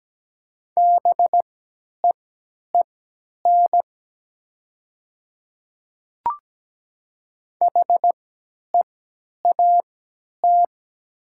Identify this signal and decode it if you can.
Morse code, heard as a keyed tone, about 17 wpm, spaced as for 10 wpm, 700 Hz. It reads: BEEN HEAT
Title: 3x Character Spacing - Top 500 Words - 17wpm